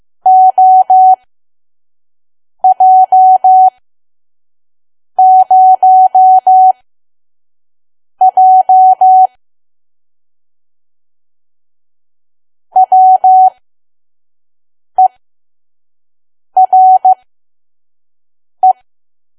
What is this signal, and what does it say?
Morse code, heard as a keyed tone, about 15 wpm, spaced as for 5 wpm, 750 Hz. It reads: OJ0J WERE
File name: Archive / 5 WPM files / PM